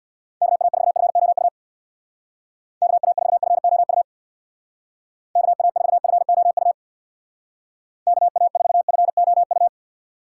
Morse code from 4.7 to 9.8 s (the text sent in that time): KA4FOU KA4FOU